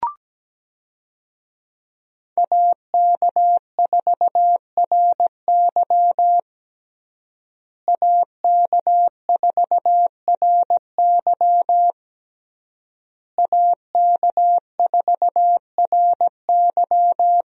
Morse code, heard as a keyed tone, about 17 wpm, 700 Hz. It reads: AK4RY AK4RY AK4RY
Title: USA Call Signs - 5 Characters - Repeated 3x - 17wpm